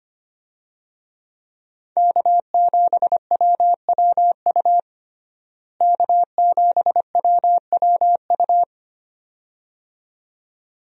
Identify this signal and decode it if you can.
Morse code, heard as a keyed tone, about 25 wpm, 700 Hz. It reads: K7WWU K7WWU
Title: USA Call Signs - 5 Characters - Repeated 2x - 25wpm